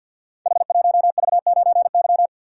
Morse code, heard as a keyed tone, about 47 wpm, 700 Hz. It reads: H1V9Y
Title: Code Groups - 5 Letter:Numbers - 50wpm